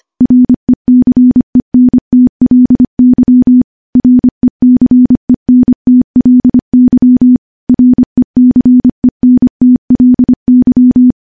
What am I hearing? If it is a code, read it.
Morse code, heard as a keyed tone, about 25 wpm, 250 Hz. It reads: RECENTLY RECENTLY RECENTLY
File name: morse / morse-training-25WPM